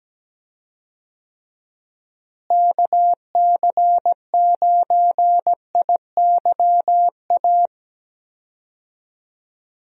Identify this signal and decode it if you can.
Morse code, heard as a keyed tone, about 17 wpm, 700 Hz. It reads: KC9IYA
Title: USA Call Signs - 6 Characters - Repeated 3x - 17wpm